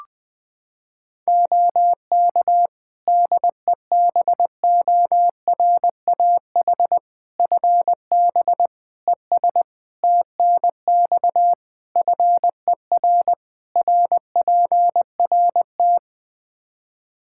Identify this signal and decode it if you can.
Morse code, heard as a keyed tone, about 21 wpm, 700 Hz. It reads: OK DEBORAH FB ES TNX FER RPRT
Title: Practice QSO - Exchange 2 - Info - 20wpm